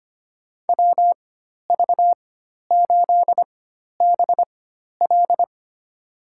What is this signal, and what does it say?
Morse code, heard as a keyed tone, about 25 wpm, spaced as for 12 wpm, 700 Hz. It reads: WV8BL
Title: USA Call Signs - 5 Characters - 4x Character Spacing - 25wpm